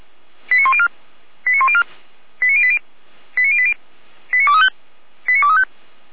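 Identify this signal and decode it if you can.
Five-tone selective call (ZVEI), heard as quick runs of five tones, tones of about 80 ms. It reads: 89104 89104 89080 89080 89235 89235
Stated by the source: ZVEI five-tone selective call received on an SDR.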